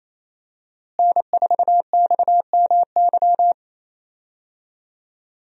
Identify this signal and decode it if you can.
Morse code, heard as a keyed tone, about 27 wpm, 700 Hz. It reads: N4XMY
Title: USA Call Signs - 5 Characters - Repeated 3x - 28wpm